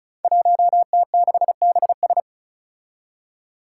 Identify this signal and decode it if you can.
Morse code, heard as a keyed tone, about 36 wpm, 700 Hz. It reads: 1T6BS
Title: Five Challenging Letter-Number Combinations 35wpm